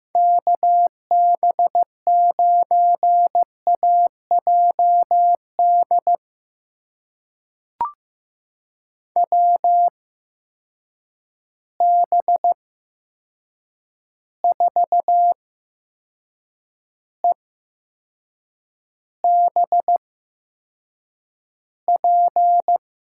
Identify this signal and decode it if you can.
Morse code, heard as a keyed tone, about 15 wpm, spaced as for 6 wpm, 700 Hz. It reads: KB9AJD WB4EBP
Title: USA Call Signs - 8x Character Spacing - 15wpm